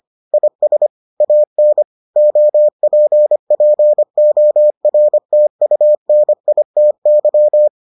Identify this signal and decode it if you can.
Morse code, heard as a keyed tone, about 25 wpm, 600 Hz. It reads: IS AN OPPORTUNITY